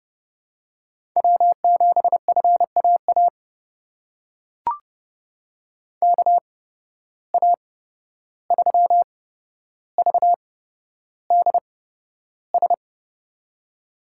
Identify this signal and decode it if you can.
Morse code, heard as a keyed tone, about 32 wpm, spaced as for 12 wpm, 700 Hz. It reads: W7FAA KA3VDS
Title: USA Call Signs - 8x Character Spacing - 30wpm